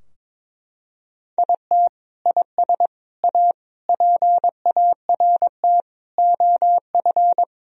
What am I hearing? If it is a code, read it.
Morse code, heard as a keyed tone, about 22 wpm, 700 Hz. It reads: IT IS A PART OF
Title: Sentences from Top 1000 Words - Review Part 4 - 22wpm